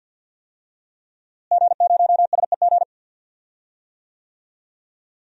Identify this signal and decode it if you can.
Morse code, heard as a keyed tone, about 48 wpm, 700 Hz. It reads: G0SEG
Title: DXCC-100 Call Signs - 5 Characters - Repeated 3x - 50wpm